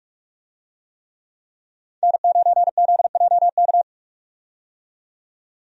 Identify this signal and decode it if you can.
Morse code, heard as a keyed tone, about 46 wpm, 700 Hz. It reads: N9ZJK
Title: USA Call Signs - 5 Characters - 45wpm